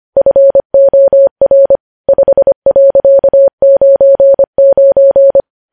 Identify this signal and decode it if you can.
Morse code, heard as a keyed tone, about 25 wpm, 550 Hz. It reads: FOR 5.99
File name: podcast / News headlines-1x-25x25